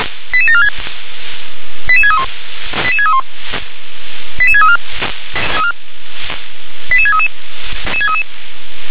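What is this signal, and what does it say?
Five-tone selective call (ZVEI), heard as quick runs of five tones, tones of about 70 ms. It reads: 88746 88631 88631 88634 88633
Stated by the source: ZVEI five-tone selective call 88731 received on an SDR.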